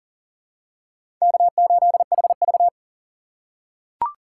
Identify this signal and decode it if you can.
Morse code, heard as a keyed tone, about 40 wpm, 700 Hz. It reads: K8HV
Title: USA Call Signs - 4 Characters - 3x Character Spacing - 40wpm